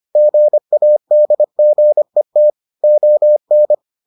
Morse code, heard as a keyed tone, about 25 wpm, 600 Hz. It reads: GADGET ON